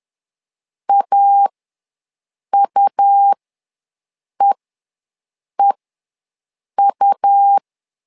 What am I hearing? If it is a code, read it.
Morse code, heard as a keyed tone, about 11 wpm, spaced as for 6 wpm, 800 Hz. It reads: AUEEU
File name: LEZIONI / 2 UHDN / 2 U ws 10wpm 2017